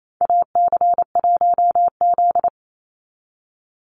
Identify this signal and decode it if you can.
Morse code, heard as a keyed tone, about 28 wpm, 700 Hz. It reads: AC1Z
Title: NA SOTA Chaser Callsigns 28wpm